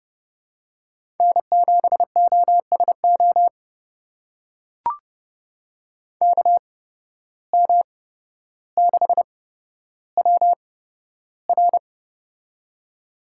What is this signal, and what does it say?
Morse code, heard as a keyed tone, about 29 wpm, spaced as for 13 wpm, 700 Hz. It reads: N7OSO KM6WR
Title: USA Call Signs - 5 Characters - 8x Character Spacing - 30wpm